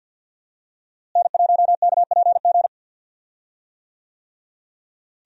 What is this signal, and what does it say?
Morse code, heard as a keyed tone, about 46 wpm, 700 Hz. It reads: N1KPG
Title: USA Call Signs - 5 Characters - Repeated 2x - 50wpm